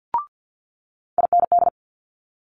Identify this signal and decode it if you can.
Morse code, heard as a keyed tone, about 50 wpm, 700 Hz. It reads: IND
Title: Multi-letter Phonemes Quick Fire 50wpm